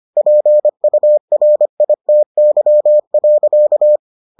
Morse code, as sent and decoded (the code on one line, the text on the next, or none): .--. ..- .-. .. - -.-- .-.-.-
PURITY.